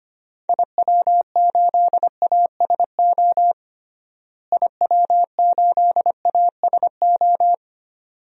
Morse code, heard as a keyed tone, about 25 wpm, 700 Hz. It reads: IW8ASO IW8ASO
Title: DXCC-100 Call Signs - 6 Characters - Repeated 2x - 25wpm